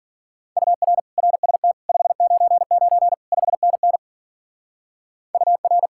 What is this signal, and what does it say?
Morse code, heard as a keyed tone, about 49 wpm, 700 Hz. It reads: UR RST 599 5NN UR
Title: Speed-Racing Common QSO Phrases Encore - 40wpm